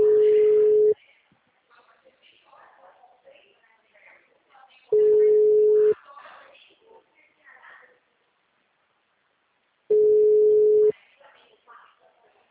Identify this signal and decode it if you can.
Ringback tone (Europe, 425 Hz), with speech.